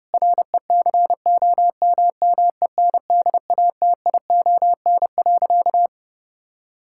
Morse code, heard as a keyed tone, about 30 wpm, 700 Hz. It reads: RECOMMENDATION.
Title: Sentences from Top 1600 Words 30wpm